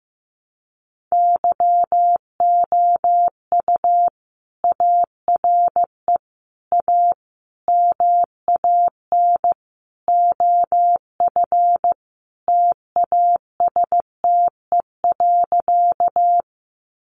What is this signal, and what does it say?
Morse code, heard as a keyed tone, about 15 wpm, 700 Hz. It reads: YOU ARE A MAN OF TASTE.
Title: Sentences from Top 1100 Words 15wpm